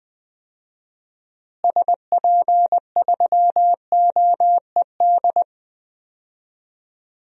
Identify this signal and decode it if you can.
Morse code, heard as a keyed tone, about 21 wpm, 700 Hz. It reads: SP3OED
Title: DXCC-100 Call Signs - 6 Characters - Repeated 3x - 20wpm